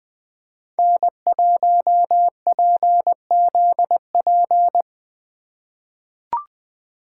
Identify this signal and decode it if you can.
Morse code, heard as a keyed tone, about 20 wpm, 700 Hz. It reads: N1PZP
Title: USA Call Signs - 5 Characters - 3x Character Spacing - 20wpm